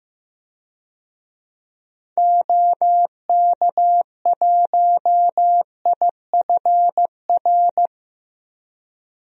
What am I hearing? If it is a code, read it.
Morse code, heard as a keyed tone, about 15 wpm, 700 Hz. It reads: OK1IFR
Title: DXCC-100 Call Signs - 6 Characters - Repeated 3x - 15wpm